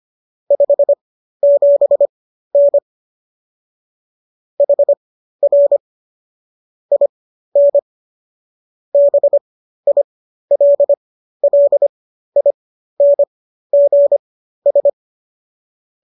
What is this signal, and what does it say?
Morse code, heard as a keyed tone, about 25 wpm, spaced as for 13 wpm, 570 Hz. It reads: 57N HR IN BILLINGS